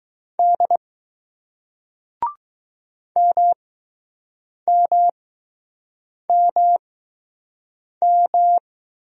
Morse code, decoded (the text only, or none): D MMMM